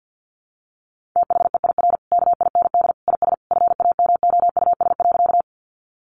Morse code, heard as a keyed tone, about 50 wpm, 700 Hz. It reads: THEIR KIND IS FAMOUS.
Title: Sentences from Top 1500 Words 50wpm